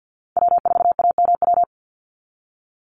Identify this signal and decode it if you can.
Morse code, heard as a keyed tone, about 48 wpm, 700 Hz. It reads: W4AMW
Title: NA SOTA Chaser Callsigns 50wpm